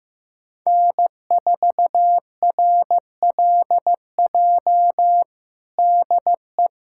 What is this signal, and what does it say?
Morse code, heard as a keyed tone, about 15 wpm, 700 Hz. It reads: N4RLJ DE
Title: USA Callsign DE Callsign - 5 Characters - 15wpm